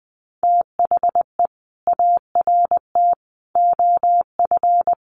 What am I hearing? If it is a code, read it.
Morse code, heard as a keyed tone, about 20 wpm, 700 Hz. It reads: THE ART OF
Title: Sentences from Top 1300 Words 20wpm